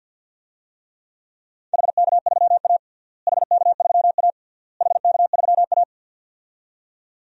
Morse code, heard as a keyed tone, about 52 wpm, 700 Hz. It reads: HK3A HK3A HK3A
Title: DXCC-100 Call Signs - 4 Characters - Repeated 3x - 50wpm